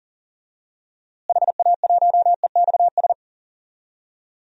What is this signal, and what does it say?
Morse code, heard as a keyed tone, about 40 wpm, 700 Hz. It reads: HA1EXS
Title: DXCC-100 Call Signs - 6 Characters - Repeated 3x - 40wpm